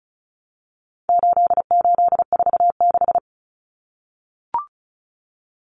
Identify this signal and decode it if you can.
Morse code, heard as a keyed tone, about 34 wpm, 700 Hz. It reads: 8846
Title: Numbers - 4-Digit 35wpm